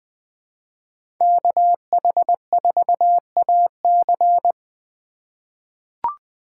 Morse code, heard as a keyed tone, about 20 wpm, 700 Hz. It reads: KH4AC
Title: USA Call Signs - 5 Characters - 2x Character Spacing - 20wpm